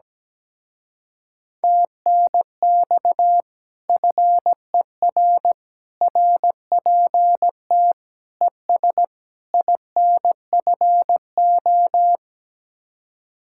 Morse code, heard as a keyed tone, about 17 wpm, 700 Hz. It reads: TNX FER RPT ES INFO
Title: Speed-Racing Common QSO Phrases Encore - 17wpm